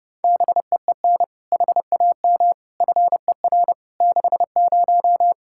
Morse code, heard as a keyed tone, about 30 wpm, 700 Hz. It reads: BEEN HAM FER 60
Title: Practice QSO - Exchange 3 - More Info - 30wpm